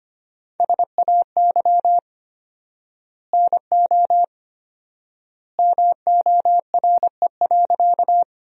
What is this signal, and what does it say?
Morse code, heard as a keyed tone, about 25 wpm, 700 Hz. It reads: SAY NO MORE.
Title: 4x Word-Spacing - Sentences from Top 200 Words - 25wpm